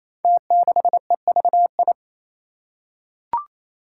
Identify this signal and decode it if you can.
Morse code, heard as a keyed tone, about 28 wpm, 700 Hz. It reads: T6EVI
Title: Five Challenging Letter-Number Combinations 28wpm